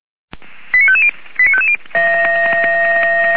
Five-tone selective call (ZVEI), heard as quick runs of five tones, tones of about 70 ms. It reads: 89550 89550